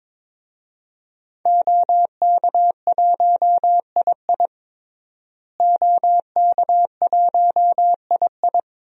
Morse code, heard as a keyed tone, about 21 wpm, 700 Hz. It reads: OK1II OK1II
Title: DXCC-100 Call Signs - 5 Characters - Repeated 3x - 22wpm